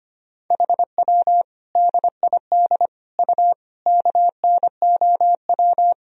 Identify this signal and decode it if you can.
Morse code, heard as a keyed tone, about 25 wpm, 700 Hz. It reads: HW DID U KNOW
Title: Sentences from Top 100 Words and Abbreviations 25wpm